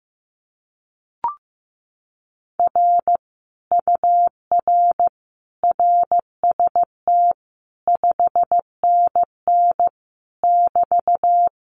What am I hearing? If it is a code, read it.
Morse code, heard as a keyed tone, about 15 wpm, 700 Hz. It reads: R UR RST 5NN <BT>
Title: Common QSO Phrases Encore 15wpm